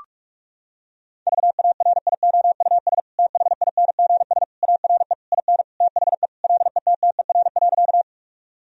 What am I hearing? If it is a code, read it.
Morse code, heard as a keyed tone, about 46 wpm, 700 Hz. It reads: VARIOUS THINGS ARE IN THE LETTER.
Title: Sentences from Top 1000 Words - Review Part 4 - 45wpm